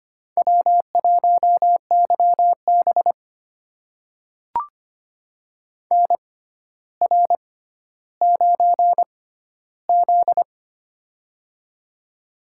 Morse code, heard as a keyed tone, about 25 wpm, spaced as for 13 wpm, 700 Hz. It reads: W1YB NR9Z
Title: USA Call Signs - 4 Characters - 6x Character Spacing - 25wpm